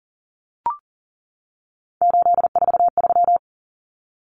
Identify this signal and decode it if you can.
Morse code, heard as a keyed tone, about 40 wpm, 700 Hz. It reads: 843
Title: Common Cut Numbers - 3-Digit 40wpm